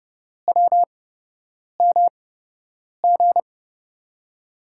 Morse code, heard as a keyed tone, about 30 wpm, spaced as for 8 wpm, 700 Hz. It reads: WMG
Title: USA Call Signs - 5 Characters - 8x Character Spacing - 30wpm